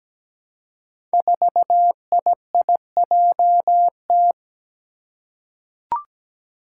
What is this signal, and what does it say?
Morse code, heard as a keyed tone, about 17 wpm, 700 Hz. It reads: 4IIJT
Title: Five Challenging Letter-Number Combinations 17wpm